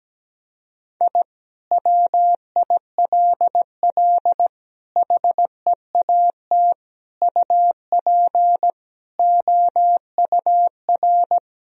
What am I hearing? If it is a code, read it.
Morse code, heard as a keyed tone, about 17 wpm, 700 Hz. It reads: I WILL HEAT UP OUR
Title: Sentences from Top 1000 Words - Review Part 3 - 17wpm